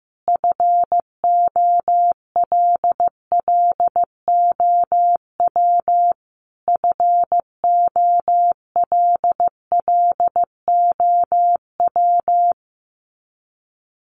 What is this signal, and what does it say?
Morse code, heard as a keyed tone, about 15 wpm, 700 Hz. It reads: FOLLOW FOLLOW
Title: Top 200 Words Repeated 15wpm